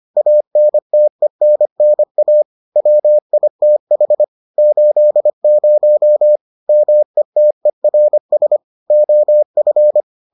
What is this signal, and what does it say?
Morse code, heard as a keyed tone, about 25 wpm, 600 Hz. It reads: ANTENNA WITH 80 METERS OF